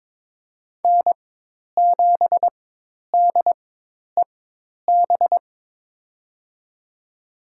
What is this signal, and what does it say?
Morse code, heard as a keyed tone, about 22 wpm, spaced as for 10 wpm, 700 Hz. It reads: N7DEB
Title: USA Call Signs - 5 Characters - 4x Character Spacing - 22wpm